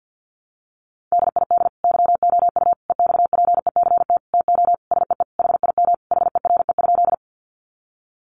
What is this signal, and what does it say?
Morse code, heard as a keyed tone, about 50 wpm, 700 Hz. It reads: DID YOU EXPECT TO SEE HIM HERE?
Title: Sentences from Top 1100 Words 50wpm